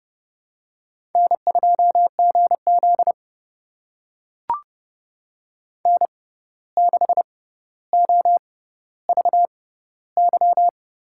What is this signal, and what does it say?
Morse code, heard as a keyed tone, about 30 wpm, spaced as for 14 wpm, 700 Hz. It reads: N2GZ N6OVY
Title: USA Call Signs - 6x Character Spacing - 30wpm